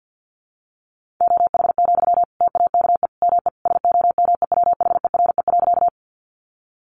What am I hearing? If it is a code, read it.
Morse code, heard as a keyed tone, about 50 wpm, 700 Hz. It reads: OH, TAKE ME SOMEWHERE.
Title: Sentences from Top 1400 Words 50wpm